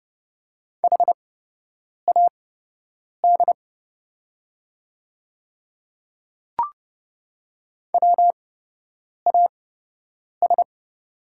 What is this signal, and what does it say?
Morse code, heard as a keyed tone, about 30 wpm, spaced as for 8 wpm, 700 Hz. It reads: HAD WAS
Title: 8x Character Spacing - Top 100 Words - 30wpm